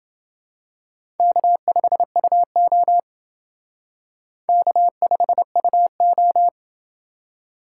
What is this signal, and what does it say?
Morse code, heard as a keyed tone, about 28 wpm, 700 Hz. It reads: K5UO K5UO